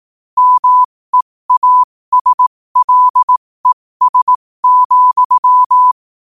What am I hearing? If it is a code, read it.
Morse code, heard as a keyed tone, about 18 wpm, spaced as for 15 wpm, 1000 Hz. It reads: MEASLES,
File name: qotdfiles / 15wpm